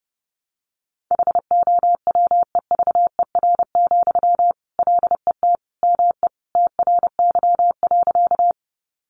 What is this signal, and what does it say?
Morse code, heard as a keyed tone, about 30 wpm, 700 Hz. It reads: HOWEVER, LET ME TRY.
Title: Sentences from Top 600 Words 30wpm